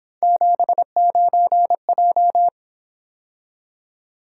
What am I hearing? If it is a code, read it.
Morse code, heard as a keyed tone, about 26 wpm, 700 Hz. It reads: 79J
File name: Speed-Racing - Three Letter-Number Combinations - 22wpm